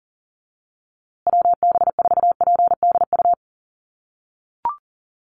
Morse code, heard as a keyed tone, about 40 wpm, 700 Hz. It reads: WB4PDU
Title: USA Call Signs Repeated 2x 40wpm